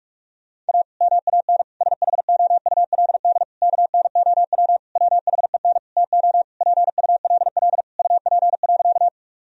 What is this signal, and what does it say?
Morse code, heard as a keyed tone, about 48 wpm, 700 Hz. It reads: A MAN SHOULD KNOW WHEN TO PULL UP.
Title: Sentences from Top 1000 Words - Review Part 2 - 45wpm